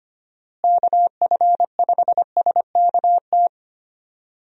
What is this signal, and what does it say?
Morse code, heard as a keyed tone, about 25 wpm, 700 Hz. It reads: KF5SKT
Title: USA Call Signs - 6 Characters - Repeated 2x - 25wpm